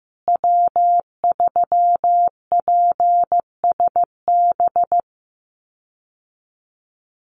Morse code, heard as a keyed tone, about 15 wpm, 700 Hz. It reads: W3PSB
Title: USA Call Signs Repeated 2x 15wpm